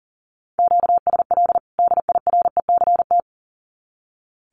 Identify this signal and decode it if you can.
Morse code, heard as a keyed tone, about 40 wpm, 700 Hz. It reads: QSL DIRECT
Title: Common QSO Phrases 40wpm